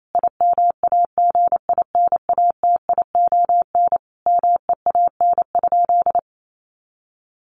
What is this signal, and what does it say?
Morse code, heard as a keyed tone, about 28 wpm, 700 Hz. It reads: IMAGINATION MEAN?
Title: Sentences from Top 1300 Words 28wpm